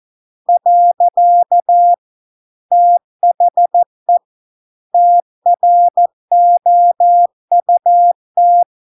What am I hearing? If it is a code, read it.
Morse code, heard as a keyed tone, about 14 wpm, 700 Hz. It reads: . THE TROUT